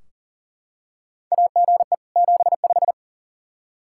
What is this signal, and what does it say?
Morse code, heard as a keyed tone, about 40 wpm, 700 Hz. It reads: AGE 75
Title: K1USN SST QSO Practice 40wpm